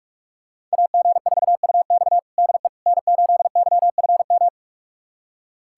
Morse code, heard as a keyed tone, about 41 wpm, 700 Hz. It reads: AG4UX DE N8YFM
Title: USA Callsign DE Callsign - 5 Characters - 45wpm